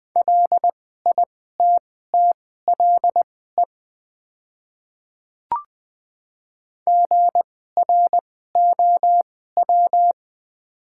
Morse code, heard as a keyed tone, about 20 wpm, spaced as for 14 wpm, 700 Hz. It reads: LITTLE GROW
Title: 2x Character Spacing - Top 300 Words - 20wpm